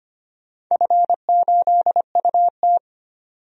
Morse code, heard as a keyed tone, about 25 wpm, 700 Hz. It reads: F8UT
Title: DXCC-100 Call Signs - 4 Characters - 25wpm